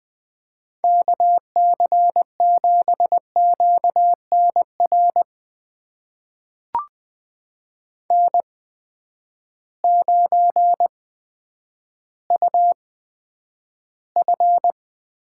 Morse code, heard as a keyed tone, about 20 wpm, spaced as for 10 wpm, 700 Hz. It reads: KC7QNR N9UF
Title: USA Call Signs - 8x Character Spacing - 20wpm